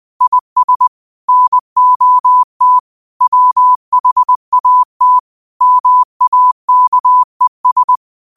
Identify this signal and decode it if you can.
Morse code, heard as a keyed tone, about 20 wpm, 1000 Hz. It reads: IS NOT WHAT MAKES